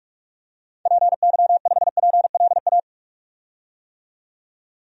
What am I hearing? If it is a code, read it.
Morse code, heard as a keyed tone, about 46 wpm, 700 Hz. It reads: PY5PLA
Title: DXCC-100 Call Signs - 6 Characters - 45wpm